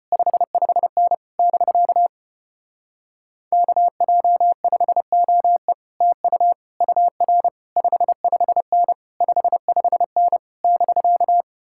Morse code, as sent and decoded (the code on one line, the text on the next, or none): ..... ..... -. / -...-.- / -.- .--- ..... --- . / - ..- / ..- .-. / ..... ..... -. / ..... ..... -. / -...-.-
55N <BK> KJ5OE TU UR 55N 55N <BK>